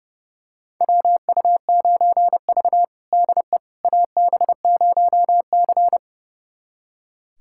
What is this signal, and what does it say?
Morse code, heard as a keyed tone, about 32 wpm, 700 Hz. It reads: WU9V DE AB0C
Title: USA Callsign DE Callsign - 4 Characters - 30wpm